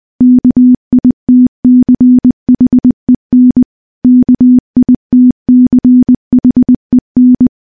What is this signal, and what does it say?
Morse code, heard as a keyed tone, about 20 wpm, 250 Hz. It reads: KITCHEN KITCHEN
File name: morse / morse-training-20WPM